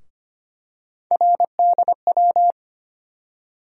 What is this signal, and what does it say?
Morse code, heard as a keyed tone, about 25 wpm, 700 Hz. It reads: RDW
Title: USA Call Sign Postfixes - 3 Letters 25wpm